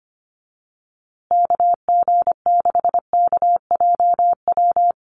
Morse code, heard as a keyed tone, about 25 wpm, 700 Hz. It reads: KG6KJW
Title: USA Call Signs Repeated 2x 25wpm